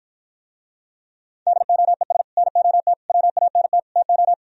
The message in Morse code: DOES NOT WANT TO

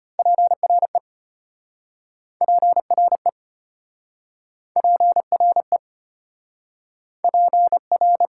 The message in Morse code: PRE PRE PRE PR

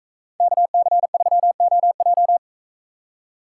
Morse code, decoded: KC3OJ

